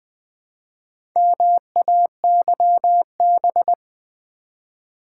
Morse code, heard as a keyed tone, about 20 wpm, 700 Hz. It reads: MAYB
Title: Build A Word - 1 Letter At A Time - 3 to 5-Letter Words - 20wpm